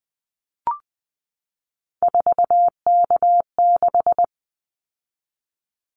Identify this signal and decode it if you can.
Morse code, heard as a keyed tone, about 19 wpm, 700 Hz. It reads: 4K6
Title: Callsign Prefixes for Top 100 DXCC - 20wpm